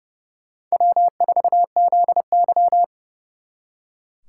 Morse code, heard as a keyed tone, about 38 wpm, 700 Hz. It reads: W4ZY